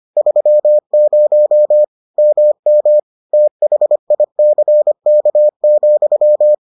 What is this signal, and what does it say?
Morse code, heard as a keyed tone, about 25 wpm, 600 Hz. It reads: 30 MM THICK,